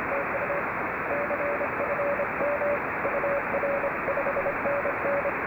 Morse code, heard as a keyed tone, about 28 wpm, 560 Hz, with radio noise. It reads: TU CFM UR5NN
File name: RS44-A65BR